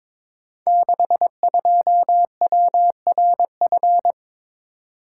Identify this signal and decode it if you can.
Morse code, heard as a keyed tone, about 22 wpm, 700 Hz. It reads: 62WRF
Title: Code Groups - 5 Letter:Numbers - 22wpm